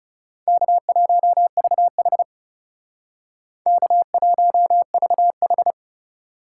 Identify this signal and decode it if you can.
Morse code, heard as a keyed tone, about 35 wpm, 700 Hz. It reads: K1VH K1VH
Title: Speed Racing - USA 1x2 Call Signs 30wpm